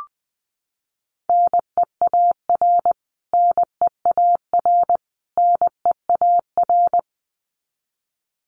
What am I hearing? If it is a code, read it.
Morse code, heard as a keyed tone, about 20 wpm, 700 Hz. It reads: NEAR NEAR NEAR
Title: Top 500 Words Repeated 20wpm